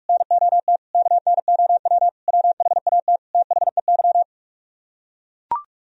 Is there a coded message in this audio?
Yes: Morse code from 0.1 to 4.2 s, heard as a keyed tone, about 45 wpm, 700 Hz. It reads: NOT KNOW WHAT THEY